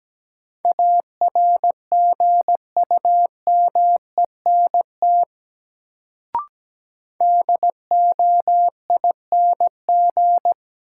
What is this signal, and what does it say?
Morse code, heard as a keyed tone, about 17 wpm, 700 Hz. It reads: ARGUMENT DOING
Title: Sentences from Top 1400 Words - New Word Drill - 17wpm